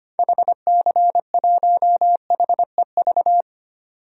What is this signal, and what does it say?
Morse code, heard as a keyed tone, about 25 wpm, 700 Hz. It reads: HC1HEV
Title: DXCC-100 Call Signs - 6 Characters - 25wpm